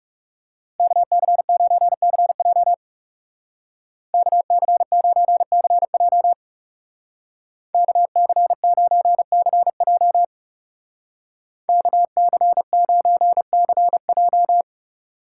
Morse code, heard as a keyed tone, about 37 wpm, 700 Hz. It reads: KC9CJ KC9CJ KC9CJ KC9CJ
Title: Speed Racing - USA 2x2 Call Signs 30wpm